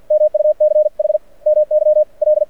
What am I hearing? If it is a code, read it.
Morse code, heard as a keyed tone, about 52 wpm, 600 Hz. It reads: MUCH MY L
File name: archived / CRSnet